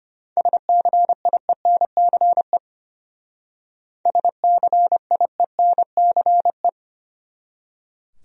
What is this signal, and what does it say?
Morse code, heard as a keyed tone, about 31 wpm, 700 Hz. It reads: SCIENCE SCIENCE
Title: Speed-Racing Top 500 Words 25wpm